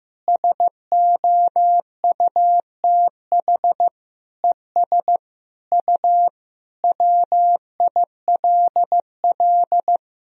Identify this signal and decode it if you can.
Morse code, heard as a keyed tone, about 15 wpm, 700 Hz. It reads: SOUTH ES U WILL
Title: Sentences from Top 400 Words and Abbreviations 15wpm